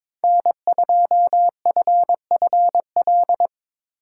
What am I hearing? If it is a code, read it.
Morse code, heard as a keyed tone, about 24 wpm, 700 Hz. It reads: N2FFL
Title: Speed Racing - USA 1x3 Call Signs 22wpm